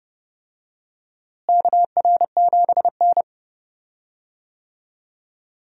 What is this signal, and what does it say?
Morse code, heard as a keyed tone, about 30 wpm, 700 Hz. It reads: KR7N
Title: USA Call Signs - 4 Characters - Repeated 2x - 30wpm